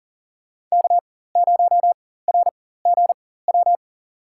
Morse code, heard as a keyed tone, about 40 wpm, spaced as for 19 wpm, 700 Hz. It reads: K0RGW
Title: USA Call Signs - 5 Characters - 4x Character Spacing - 40wpm